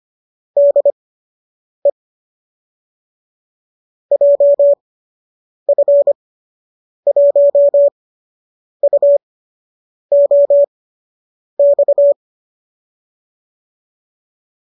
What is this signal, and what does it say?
Morse code, heard as a keyed tone, about 25 wpm, spaced as for 8 wpm, 570 Hz. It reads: DE JF1UOX